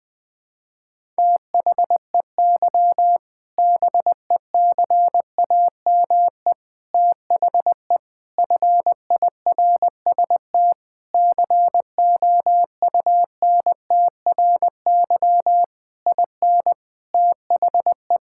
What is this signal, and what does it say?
Morse code, heard as a keyed tone, about 20 wpm, 700 Hz. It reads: THEY BECAME THE FIRST COUNTRY IN THE
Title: Sentences with 8-Word Phrases - Spoken Beginning - 20wpm